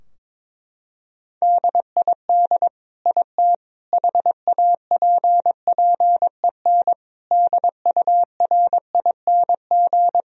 Morse code, heard as a keyed tone, about 21 wpm, 700 Hz. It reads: DID IT HAPPEN DURING
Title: Sentences from Top 1000 Words - Review Part 2 - 22wpm